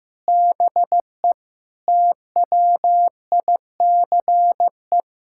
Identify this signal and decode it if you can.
Morse code, heard as a keyed tone, about 15 wpm, 700 Hz. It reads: BE TWICE
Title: Sentences from Top 1600 Words 15wpm